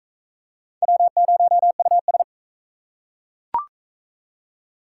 Morse code, heard as a keyed tone, about 41 wpm, 700 Hz. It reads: W0US